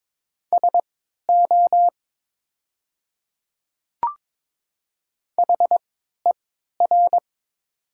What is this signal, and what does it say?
Morse code, heard as a keyed tone, about 22 wpm, spaced as for 13 wpm, 700 Hz. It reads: SO HER
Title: 3x Character Spacing - Top 200 Words - 22wpm